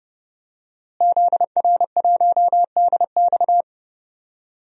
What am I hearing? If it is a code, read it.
Morse code, heard as a keyed tone, about 30 wpm, 700 Hz. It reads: ZR1DX
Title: DXCC-100 Call Signs - 5 Characters - Repeated 3x - 30wpm